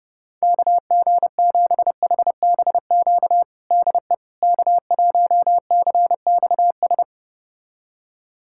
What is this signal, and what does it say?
Morse code, heard as a keyed tone, about 32 wpm, 700 Hz. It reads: KG7HBQ DE K1CXS
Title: USA Callsign DE Callsign 30wpm